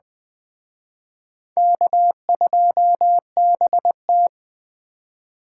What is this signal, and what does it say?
Morse code, heard as a keyed tone, about 21 wpm, 700 Hz. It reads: K2BT